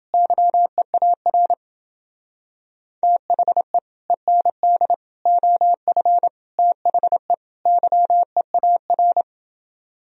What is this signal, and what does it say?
Morse code, heard as a keyed tone, about 28 wpm, 700 Hz. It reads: YEAR THE END OF THE YEAR